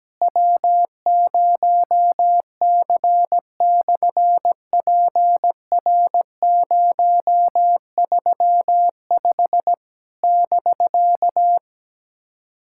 Morse code, heard as a keyed tone, about 17 wpm, 700 Hz. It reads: W0C/PR035 <BK>
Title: SOTA QSO Practice 17wpm